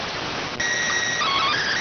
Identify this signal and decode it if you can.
SSTV (slow-scan television), PD 290.